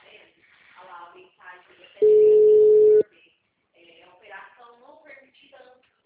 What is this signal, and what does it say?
Ringback tone (Europe, 425 Hz), with speech.
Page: Goiania - GO